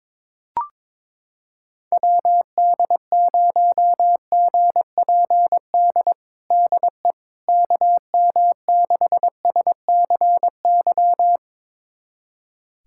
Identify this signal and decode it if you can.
Morse code, heard as a keyed tone, about 22 wpm, 700 Hz. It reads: WD0GPD DE KM6SCY
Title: USA Callsign DE Callsign 22wpm